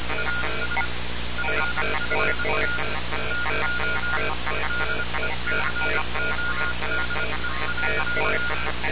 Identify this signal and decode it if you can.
Morse code, heard as a keyed tone, about 25 wpm, 1430 Hz, with noise. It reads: AS FFT TO DECODE.
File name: radiocap / REC z